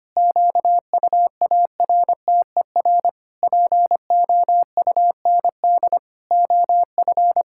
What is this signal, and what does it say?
Morse code, heard as a keyed tone, about 25 wpm, 700 Hz. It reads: QUARTER POUND OF
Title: Pangram Challenge v2 - 25wpm